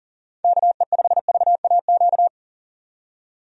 Morse code, heard as a keyed tone, about 40 wpm, 700 Hz. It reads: KE5VAQ
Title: USA Call Signs - 6 Characters - 40wpm